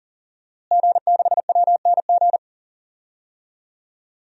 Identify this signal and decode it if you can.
Morse code, heard as a keyed tone, about 40 wpm, 700 Hz. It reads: G6WNG
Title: DXCC-100 Call Signs - 5 Characters - Repeated 3x - 40wpm